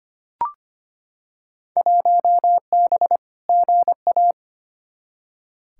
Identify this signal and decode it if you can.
Morse code, heard as a keyed tone, about 25 wpm, 700 Hz. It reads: 1B GA
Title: Field Day Exchanges Encore - 25wpm